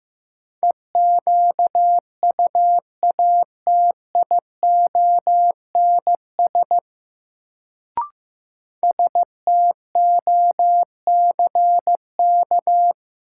Morse code, heard as a keyed tone, about 15 wpm, 700 Hz. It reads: EQUATIONS STOCK
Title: Sets of 5 Words - New Word Drill - 15wpm